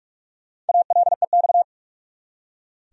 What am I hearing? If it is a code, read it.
Morse code, heard as a keyed tone, about 43 wpm, 700 Hz. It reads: ALEX
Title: USA Common Names - Max 5 Letters - 45wpm